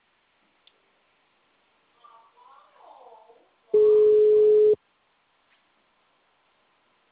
Ringback tone (Europe, 425 Hz).
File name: out- -2015- - - .32889